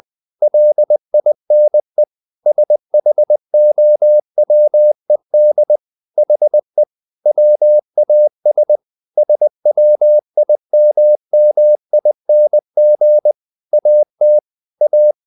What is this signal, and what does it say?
Morse code, heard as a keyed tone, about 20 wpm, 600 Hz. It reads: LINE SHOWED HE WAS SWIMMING AT A
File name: the old man and the sea / the old man and the sea part 1 20wpm